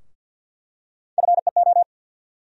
Morse code, heard as a keyed tone, about 47 wpm, 700 Hz. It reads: FEO
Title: USA Call Sign Postfixes - 3 Letters 50wpm